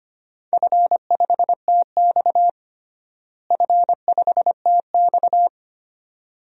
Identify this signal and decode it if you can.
Morse code, heard as a keyed tone, about 25 wpm, 700 Hz. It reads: F5TX F5TX
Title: DXCC-100 Call Signs - 4 Characters - Repeated 2x - 25wpm